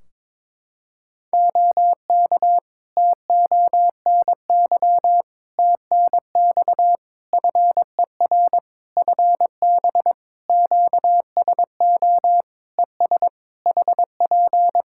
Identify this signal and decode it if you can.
Morse code, heard as a keyed tone, about 22 wpm, 700 Hz. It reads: OK TONY TNX FER FB QSO ES HP
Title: Practice QSO - Exchange 4 - Ending - 22wpm